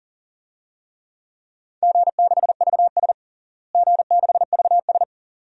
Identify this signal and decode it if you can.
Morse code, heard as a keyed tone, about 40 wpm, 690 Hz. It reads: G6VS G6VS
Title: DXCC-100 Call Signs - 4 Characters - Repeated 2x - 40wpm